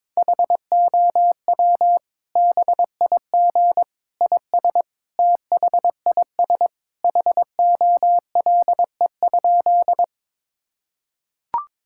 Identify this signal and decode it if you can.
Morse code, heard as a keyed tone, about 21 wpm, 700 Hz. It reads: HOW BIG IS THIS HOLE?
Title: Sentences from Top 1600 Words 22wpm